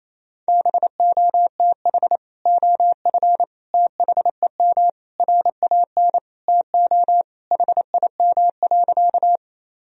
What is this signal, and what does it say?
Morse code, heard as a keyed tone, about 28 wpm, 700 Hz. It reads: BOTH OF THEM RAN TO HIM.
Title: Sentences from Top 1000 Words - Review Part 3 - 28wpm